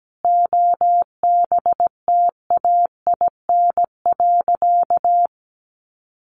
Morse code, heard as a keyed tone, about 17 wpm, 700 Hz. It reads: OBTAIN.
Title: Sentences from Top 1300 Words 17wpm